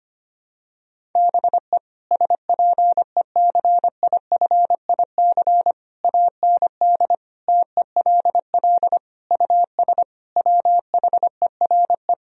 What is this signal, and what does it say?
Morse code, heard as a keyed tone, about 25 wpm, 700 Hz. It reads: BE SPECIFIC AND TELL US WHERE
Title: Sentences from Top 1000 Words - Review Part 3 - 25wpm